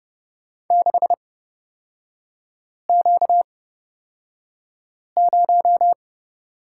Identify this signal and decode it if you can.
Morse code, heard as a keyed tone, about 30 wpm, spaced as for 5 wpm, 700 Hz. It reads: 6Q0
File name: Lesson 38e - Letter q - Three Characters 30wpm